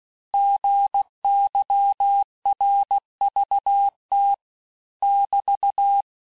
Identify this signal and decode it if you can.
Morse code, heard as a keyed tone, about 16 wpm, 800 Hz. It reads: GYRVT <BT>
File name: cours-du-lundi files / Cours-1586-texte-chiffre-vitesse-15mpm